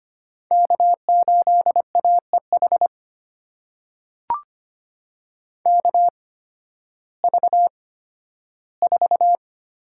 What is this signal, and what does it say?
Morse code, heard as a keyed tone, about 25 wpm, spaced as for 13 wpm, 700 Hz. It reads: K8AEH KV4
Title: USA Call Signs - 8x Character Spacing - 25wpm